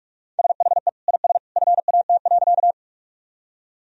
Morse code, heard as a keyed tone, about 46 wpm, 700 Hz. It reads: SHE IS FAT.
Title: Sentences from Top 1000 Words - Review Part 4 - 45wpm